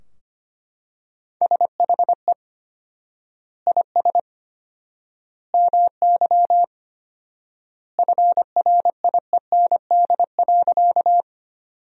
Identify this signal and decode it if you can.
Morse code, heard as a keyed tone, about 25 wpm, 700 Hz. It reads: SHE IS MY FRIEND.
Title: 4x Word-Spacing - Sentences from Top 300 Words - 25wpm